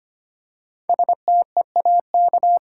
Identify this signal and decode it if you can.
Morse code, heard as a keyed tone, about 25 wpm, 700 Hz. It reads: STEAK
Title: Five Letter Words - 25wpm